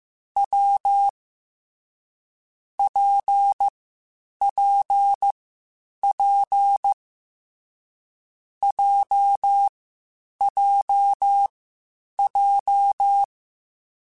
Morse code, heard as a keyed tone, about 15 wpm, 800 Hz. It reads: W P P P J J J
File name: Difficult Combos / Difficult Combos-mp3s / WPJ WPJ1 @13